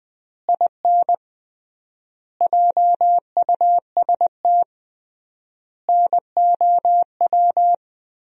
Morse code, heard as a keyed tone, about 20 wpm, 700 Hz. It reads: IN JUST NOW
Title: 3x Word-Spacing - Sentences from Top 300 Words - 20wpm